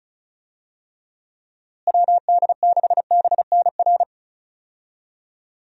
Morse code, heard as a keyed tone, about 34 wpm, 700 Hz. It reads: WD6BNR
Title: USA Call Signs - 6 Characters - Repeated 3x - 35wpm